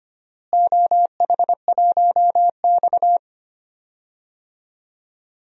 Morse code, heard as a keyed tone, about 25 wpm, 700 Hz. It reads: OH1X
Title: DXCC-100 Call Signs - 4 Characters - Repeated 3x - 25wpm